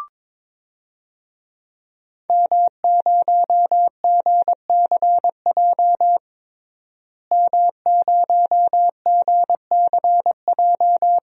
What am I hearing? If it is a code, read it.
Morse code, heard as a keyed tone, about 22 wpm, 700 Hz. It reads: M0GCJ M0GCJ
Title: DXCC-100 Call Signs - 5 Characters - Repeated 2x - 22wpm